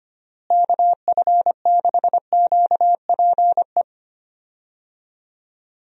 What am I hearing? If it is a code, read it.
Morse code, heard as a keyed tone, about 25 wpm, 700 Hz. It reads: KF6QPE
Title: USA Call Signs - 6 Characters - Repeated 3x - 25wpm